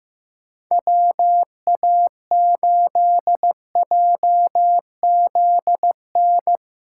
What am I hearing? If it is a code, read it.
Morse code, heard as a keyed tone, about 15 wpm, 700 Hz. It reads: WA8JZN
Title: USA Callsign DE Callsign - 6 Characters - 15wpm